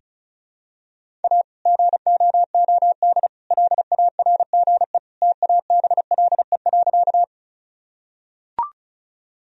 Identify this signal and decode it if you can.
Morse code, heard as a keyed tone, about 34 wpm, 700 Hz. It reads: A GOOD LARGE TABLE.
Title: Sentences from Top 1000 Words - Review Part 2 - 35wpm